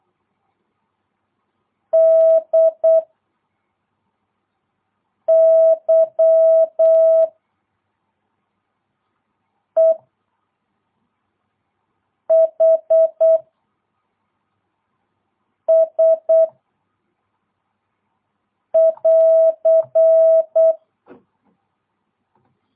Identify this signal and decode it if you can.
Morse code, heard as a keyed tone, about 8 wpm, 660 Hz. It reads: D Y E H S <AR>
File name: jh8cbh / 2022cw